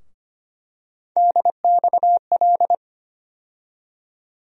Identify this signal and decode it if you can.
Morse code, heard as a keyed tone, about 25 wpm, 700 Hz. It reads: DXL
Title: USA Call Sign Postfixes - 3 Letters 25wpm